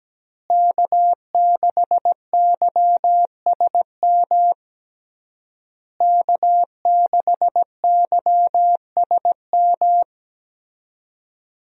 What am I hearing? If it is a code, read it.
Morse code, heard as a keyed tone, about 17 wpm, 700 Hz. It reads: K6YSM K6YSM
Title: USA Call Signs - 5 Characters - Repeated 3x - 17wpm